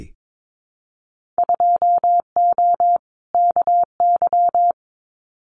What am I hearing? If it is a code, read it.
Morse code, heard as a keyed tone, about 22 wpm, 700 Hz. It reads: 2O KY
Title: Winter Field Day Exchanges 22wpm